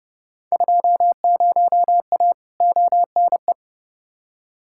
Morse code, heard as a keyed tone, about 31 wpm, 700 Hz. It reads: 20A ONE
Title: Field Day Exchanges Encore - 30wpm